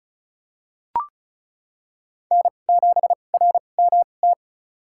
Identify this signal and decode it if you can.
Morse code, heard as a keyed tone, about 36 wpm, spaced as for 25 wpm, 700 Hz. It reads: N7RMT
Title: USA Call Signs - 2x Character Spacing - 35wpm